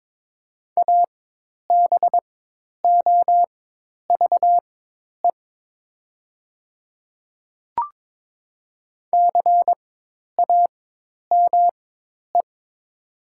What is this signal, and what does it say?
Morse code, heard as a keyed tone, about 23 wpm, spaced as for 10 wpm, 700 Hz. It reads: ABOVE CAME
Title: 4x Character Spacing - Top 500 Words - 22wpm